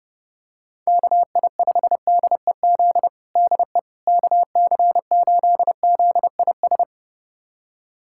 Morse code, heard as a keyed tone, about 30 wpm, 700 Hz. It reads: KI5DEZ DE KC8ZIS
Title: USA Callsign DE Callsign 30wpm